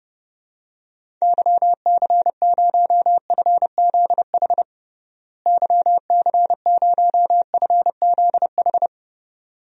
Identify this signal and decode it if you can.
Morse code, heard as a keyed tone, about 31 wpm, 700 Hz. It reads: YC0FZH YC0FZH
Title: DXCC-100 Call Signs - 6 Characters - Repeated 2x - 30wpm